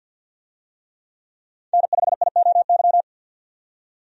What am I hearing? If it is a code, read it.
Morse code, heard as a keyed tone, about 52 wpm, 700 Hz. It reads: N5IOY
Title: USA Call Signs - 5 Characters - 50wpm